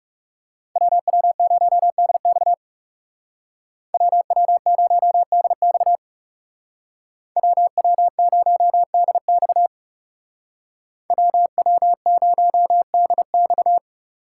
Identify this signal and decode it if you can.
Morse code, heard as a keyed tone, about 35 wpm, 700 Hz. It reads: WW0DX WW0DX WW0DX WW0DX
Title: Speed Racing - USA 2x2 Call Signs 30wpm